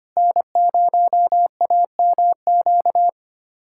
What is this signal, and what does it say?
Morse code, heard as a keyed tone, about 25 wpm, 700 Hz. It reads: N0AMQ
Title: USA Callsign DE Callsign 25wpm